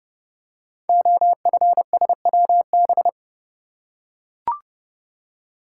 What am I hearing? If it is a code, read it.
Morse code, heard as a keyed tone, about 30 wpm, 700 Hz. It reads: OFSWB
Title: Code Groups - 5 Letters - 30wpm